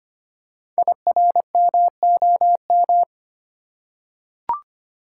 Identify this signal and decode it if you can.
Morse code, heard as a keyed tone, about 25 wpm, 700 Hz. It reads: IRMOM
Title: Code Groups - 5 Letters - 25wpm